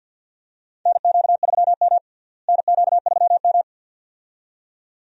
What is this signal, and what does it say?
Morse code, heard as a keyed tone, about 49 wpm, 690 Hz. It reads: NQ3M NQ3M
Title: USA Call Signs - 4 Characters - Repeated 2x - 50wpm